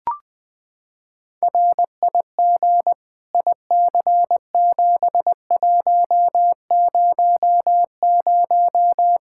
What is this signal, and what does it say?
Morse code, heard as a keyed tone, about 20 wpm, 700 Hz. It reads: RIG IC7100
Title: Practice QSO - Exchange 2 - Info - 20wpm